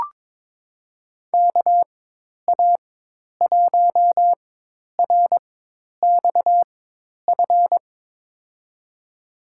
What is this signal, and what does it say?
Morse code, heard as a keyed tone, about 22 wpm, spaced as for 10 wpm, 700 Hz. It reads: KA1RXF
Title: USA Call Signs - 6 Characters - 4x Character Spacing - 22wpm